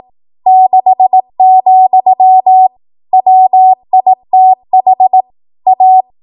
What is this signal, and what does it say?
Morse code, heard as a keyed tone, about 18 wpm, 750 Hz. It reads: 6, WITH A
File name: Morse / 18WPM